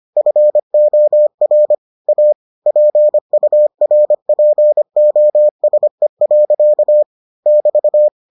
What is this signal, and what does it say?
Morse code, heard as a keyed tone, about 25 wpm, 600 Hz. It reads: FOR A PURPOSE. <BT>